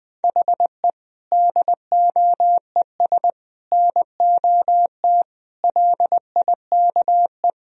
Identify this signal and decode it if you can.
Morse code, heard as a keyed tone, about 20 wpm, 700 Hz. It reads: HE DOES NOT LIKE
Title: Sentences from Top 1000 Words - Review Part 3 - 20wpm